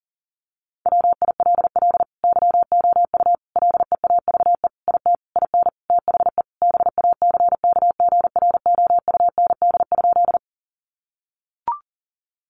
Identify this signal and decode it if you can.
Morse code, heard as a keyed tone, about 40 wpm, 700 Hz. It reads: WILL YOU LEAVE IT IN THE BACKGROUND?
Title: Sentences from Top 1300 Words 40wpm